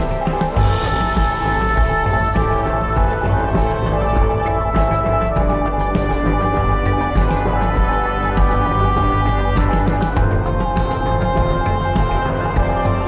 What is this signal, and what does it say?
Morse code, heard as a keyed tone, about 21 wpm, 820 Hz. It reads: WORRY. EVERYTHING YOU